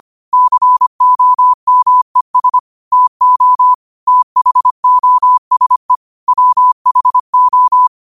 Morse code, heard as a keyed tone, about 25 wpm, 1000 Hz. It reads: COMES TO THOSE WHO